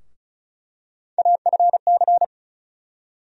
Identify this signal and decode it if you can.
Morse code, heard as a keyed tone, about 34 wpm, 700 Hz. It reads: AFC
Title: USA Call Sign Postfixes - 3 Letters 35wpm